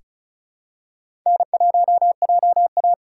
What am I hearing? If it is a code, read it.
Morse code, heard as a keyed tone, about 36 wpm, 700 Hz. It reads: N1JA
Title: USA Call Signs - 4 Characters - 4x Character Spacing - 35wpm